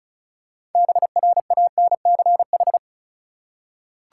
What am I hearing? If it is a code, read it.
Morse code, heard as a keyed tone, about 34 wpm, 700 Hz. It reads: BRANCH
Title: Six Letter Words - 35wpm